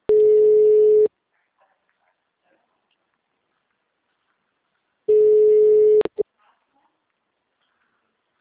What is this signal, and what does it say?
Ringback tone (Europe, 425 Hz), with speech.
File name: out- -2024- - - .2568